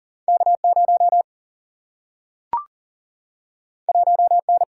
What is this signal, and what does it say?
Morse code, heard as a keyed tone, about 39 wpm, 700 Hz. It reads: K 0 1 N
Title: Single-Letter Single-Number 40wpm